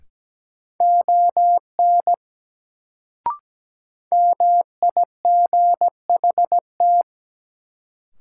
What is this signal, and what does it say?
Morse code, heard as a keyed tone, about 17 wpm, 700 Hz. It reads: ON MIGHT
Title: Sets of 2 Words - New Word Drill - 17wpm